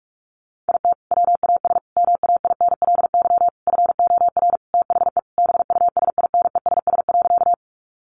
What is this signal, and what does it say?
Morse code, heard as a keyed tone, about 44 wpm, 700 Hz. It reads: IT WAS MAINLY FOR THE BUSINESS.
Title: Sentences from Top 1400 Words 45wpm